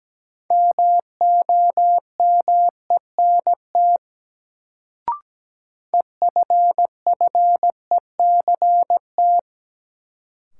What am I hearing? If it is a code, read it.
Morse code, heard as a keyed tone, about 17 wpm, 700 Hz. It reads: MOMENT EFFECT
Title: Sentences from Top 1000 Words - New Word Drill - 17wpm